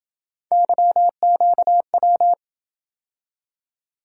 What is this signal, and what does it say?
Morse code, heard as a keyed tone, about 27 wpm, 700 Hz. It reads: YQW